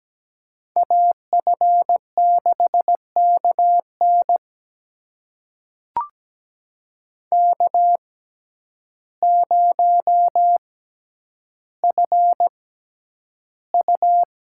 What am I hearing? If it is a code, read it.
Morse code, heard as a keyed tone, about 17 wpm, spaced as for 9 wpm, 700 Hz. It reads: AF6KN K0FU
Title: USA Call Signs - 6x Character Spacing - 17wpm